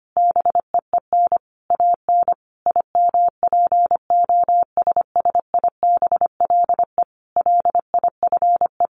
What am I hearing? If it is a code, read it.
Morse code, heard as a keyed tone, about 25 wpm, 700 Hz. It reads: BEEN AN IMPOSSIBLE LIFE
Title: Sentences from Top 1100 Words 25wpm